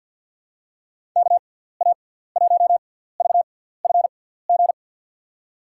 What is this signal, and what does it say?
Morse code, heard as a keyed tone, about 53 wpm, spaced as for 17 wpm, 700 Hz. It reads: KA1VFG
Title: USA Call Signs - 6x Character Spacing - 50wpm